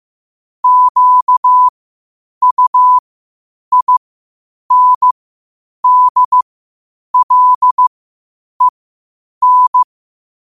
Morse code, heard as a keyed tone, about 15 wpm, spaced as for 8 wpm, 1000 Hz. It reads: QUINDLEN
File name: qotdfiles / 8wpm